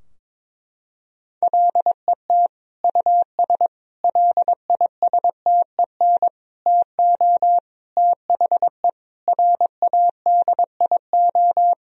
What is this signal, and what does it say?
Morse code, heard as a keyed tone, about 22 wpm, 700 Hz. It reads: LET US LISTEN TO THE RADIO
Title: Sentences from Top 1000 Words - Review Part 3 - 22wpm